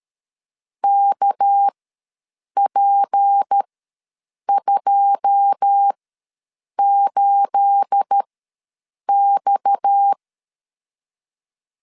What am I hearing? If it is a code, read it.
Morse code, heard as a keyed tone, about 13 wpm, spaced as for 7 wpm, 800 Hz. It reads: KP28X
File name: LEZIONI / 9 28 ws 12wpm 3354